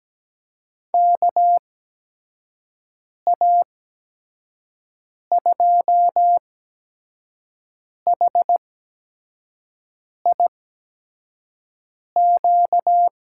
Morse code, heard as a keyed tone, about 17 wpm, spaced as for 5 wpm, 700 Hz. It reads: KA2HIQ
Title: USA Call Signs - 6 Characters - 8x Character Spacing - 17wpm